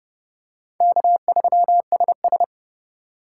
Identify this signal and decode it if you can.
Morse code, heard as a keyed tone, about 28 wpm, 700 Hz. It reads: K3SS